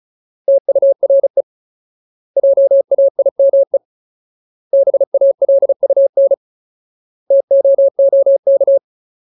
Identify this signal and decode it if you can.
Morse code, heard as a keyed tone, about 34 wpm, 550 Hz. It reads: TURE JAIME BALUN TOOK
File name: Ultimate Word List / 35WPM Ultimate